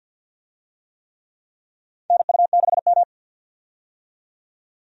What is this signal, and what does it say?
Morse code, heard as a keyed tone, about 47 wpm, 700 Hz. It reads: NU6M
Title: USA Call Signs - 4 Characters - 50wpm